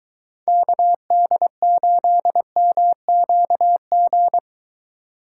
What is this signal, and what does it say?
Morse code, heard as a keyed tone, about 22 wpm, 700 Hz. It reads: KD8MQG